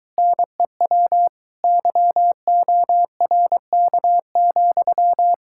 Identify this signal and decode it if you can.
Morse code, heard as a keyed tone, about 24 wpm, 700 Hz. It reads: NEW YORK,